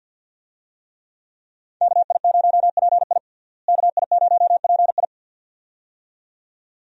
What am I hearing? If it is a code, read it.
Morse code, heard as a keyed tone, about 52 wpm, 700 Hz. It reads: KI0PI KI0PI
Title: USA Call Signs - 5 Characters - Repeated 2x - 50wpm